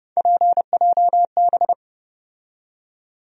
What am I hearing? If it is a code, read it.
Morse code, heard as a keyed tone, about 30 wpm, 700 Hz. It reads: PJB
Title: USA Call Sign Postfixes - 3 Letters 30wpm